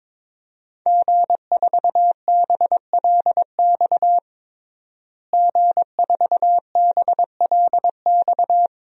Morse code, heard as a keyed tone, about 23 wpm, 700 Hz. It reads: G4BLX G4BLX
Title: DXCC-100 Call Signs - 5 Characters - Repeated 2x - 22wpm